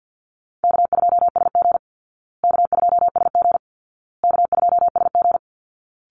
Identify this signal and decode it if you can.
Morse code, heard as a keyed tone, about 48 wpm, 700 Hz. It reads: K2SG K2SG K2SG
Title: Call Signs Repeated 2x - 50wpm